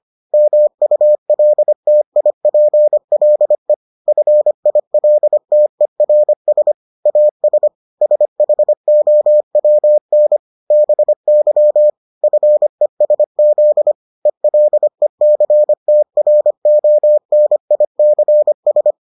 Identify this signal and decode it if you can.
Morse code, heard as a keyed tone, about 25 wpm, 600 Hz. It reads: MULTIPLE FILTERS AS SHOWN BY FESZ ELECTRONICS